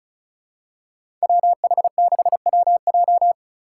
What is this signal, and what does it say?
Morse code, heard as a keyed tone, about 36 wpm, 700 Hz. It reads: WH6WJ
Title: USA Call Signs - 3x Character Spacing - 35wpm